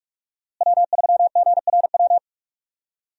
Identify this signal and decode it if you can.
Morse code, heard as a keyed tone, about 43 wpm, 700 Hz. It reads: W3GRW
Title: USA Call Signs - 5 Characters - Repeated 2x - 45wpm